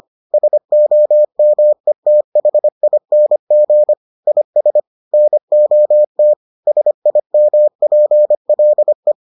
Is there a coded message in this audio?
Yes: Morse code from 0.3 to 9.1 s, heard as a keyed tone, about 25 wpm, 600 Hz. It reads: SOMETHING IS NOT SIMPLE